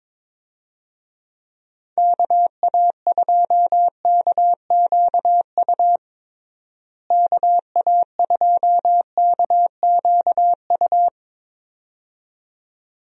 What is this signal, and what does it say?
Morse code, heard as a keyed tone, about 22 wpm, 700 Hz. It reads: KA2KQU KA2KQU
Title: USA Call Signs - 6 Characters - Repeated 2x - 22wpm